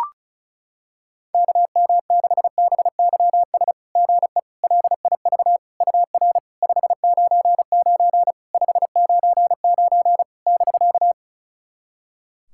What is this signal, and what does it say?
Morse code, heard as a keyed tone, about 36 wpm, 700 Hz. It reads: KM6BYS GE LIV UR 599 599 <BK>
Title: SOTA QSO Practice 35wpm